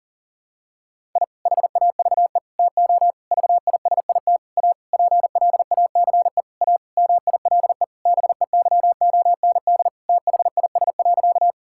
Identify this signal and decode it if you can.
Morse code, heard as a keyed tone, about 40 wpm, 700 Hz. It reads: I HAVE TO VISIT A PLACE A MILE BEYOND THIS.
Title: Sentences from Top 1000 Words - Review Part 6 - 40wpm